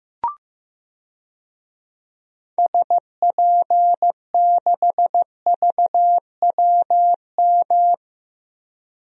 Morse code, heard as a keyed tone, about 15 wpm, 700 Hz. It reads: SP6VWM
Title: DXCC-100 Call Signs - 6 Characters - Repeated 2x - 15wpm